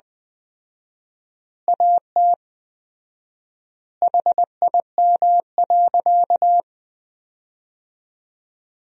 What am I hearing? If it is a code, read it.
Morse code, heard as a keyed tone, about 21 wpm, 700 Hz. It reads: AT HIM.
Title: 4x Word-Spacing - Sentences from Top 100 Words - 20wpm